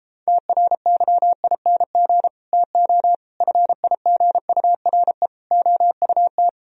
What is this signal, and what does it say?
Morse code, heard as a keyed tone, about 37 wpm, 700 Hz. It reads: TRYING TO FIGURE OUT